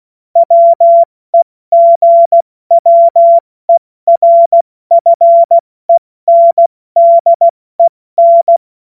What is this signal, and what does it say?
Morse code, heard as a keyed tone, about 16 wpm, spaced as for 14 wpm, 680 Hz. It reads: WEGWERFENDEN
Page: cw-witze